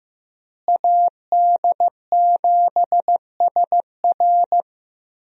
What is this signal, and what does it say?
Morse code, heard as a keyed tone, about 15 wpm, 700 Hz. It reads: AD7SR
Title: USA Call Signs - 5 Characters - Repeated 3x - 15wpm